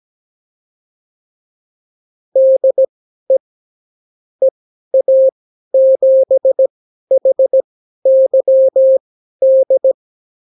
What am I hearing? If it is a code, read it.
Morse code, heard as a keyed tone, about 17 wpm, spaced as for 12 wpm, 530 Hz. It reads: DE EA7HYD
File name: CW QSO libroCW4 18-8ppm